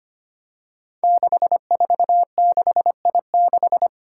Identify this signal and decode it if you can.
Morse code, heard as a keyed tone, about 25 wpm, 700 Hz. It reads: 646I6
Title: Five Challenging Letter-Number Combinations 25wpm